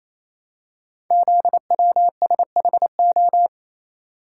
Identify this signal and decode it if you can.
Morse code, heard as a keyed tone, about 29 wpm, 700 Hz. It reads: ZWSHO
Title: Code Groups - 5 Letters - 28wpm